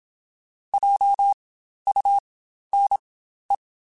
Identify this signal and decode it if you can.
Morse code, heard as a keyed tone, about 27 wpm, spaced as for 12 wpm, 800 Hz. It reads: JUNE
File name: Spaced Alphas / Spaced Alphas mp3s / Js sp @25